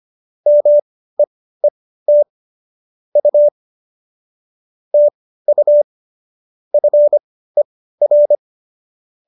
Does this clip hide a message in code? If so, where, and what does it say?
Yes: Morse code from 0.5 to 8.4 s, heard as a keyed tone, about 25 wpm, spaced as for 15 wpm, 590 Hz. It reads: MEET U TU FER